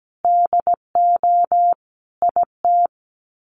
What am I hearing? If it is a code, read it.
Morse code, heard as a keyed tone, about 17 wpm, 700 Hz. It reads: DO IT
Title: Sets of 5 Words Encore2 17wpm